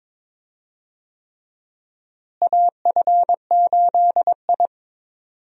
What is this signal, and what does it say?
Morse code, heard as a keyed tone, about 22 wpm, 700 Hz. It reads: AF8I
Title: USA Call Signs - 4 Characters - Repeated 3x - 22wpm